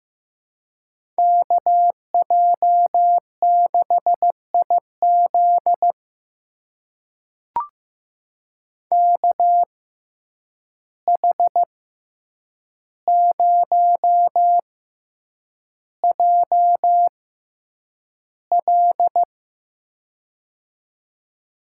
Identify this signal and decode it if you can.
Morse code, heard as a keyed tone, about 15 wpm, spaced as for 8 wpm, 700 Hz. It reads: KJ6IZ KH0JL
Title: USA Call Signs - 5 Characters - 6x Character Spacing - 15wpm